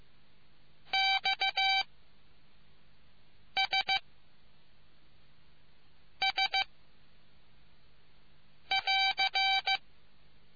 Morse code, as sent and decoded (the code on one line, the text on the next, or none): -..- / ... / ... / .-.-.
X S S <AR>